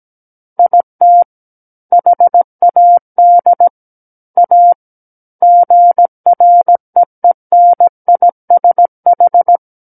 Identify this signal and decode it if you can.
Morse code, heard as a keyed tone, about 17 wpm, 700 Hz. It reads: IT HAD A GREENISH
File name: Book One / 08-Chapter 8 - 17 WPM